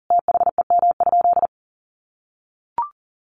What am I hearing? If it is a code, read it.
Morse code, heard as a keyed tone, about 37 wpm, 700 Hz. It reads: THEM?
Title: Sentences from Top 1000 Words 40wpm